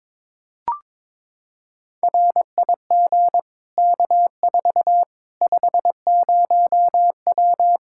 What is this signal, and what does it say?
Morse code, heard as a keyed tone, about 22 wpm, 700 Hz. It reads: RIG K4 50W
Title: Practice QSO - Exchange 2 - Info - 22wpm